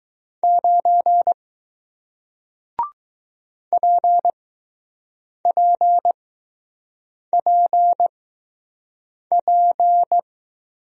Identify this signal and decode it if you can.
Morse code, heard as a keyed tone, about 20 wpm, spaced as for 6 wpm, 700 Hz. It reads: 9 PPPP